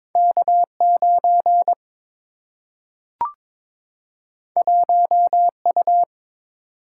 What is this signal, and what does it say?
Morse code, heard as a keyed tone, about 22 wpm, 700 Hz. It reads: K9 1U
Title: Single-Letter Single-Number 22wpm